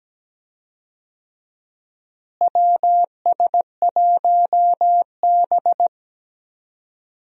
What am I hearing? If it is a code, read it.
Morse code, heard as a keyed tone, about 17 wpm, 700 Hz. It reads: WS1B
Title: USA Call Signs - 4 Characters - Repeated 2x - 17wpm